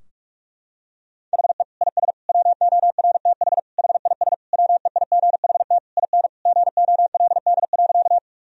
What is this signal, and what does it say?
Morse code, heard as a keyed tone, about 42 wpm, 700 Hz. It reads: HE IS WORTH HIS WEIGHT IN GOLD.
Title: Sentences from Top 1000 Words - Review Part 3 - 45wpm